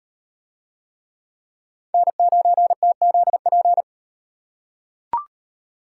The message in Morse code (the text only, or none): N9TZP